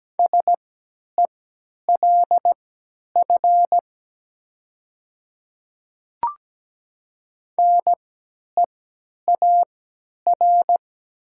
Morse code, heard as a keyed tone, about 17 wpm, spaced as for 10 wpm, 700 Hz. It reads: SELF NEAR
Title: 3x Character Spacing - Top 500 Words - 17wpm